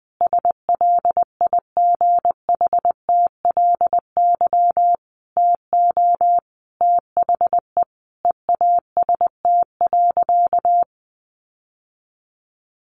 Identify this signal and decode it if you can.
Morse code, heard as a keyed tone, about 20 wpm, 700 Hz. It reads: SLIGHTLY TO THE EAST.
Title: Sentences from Top 1300 Words 20wpm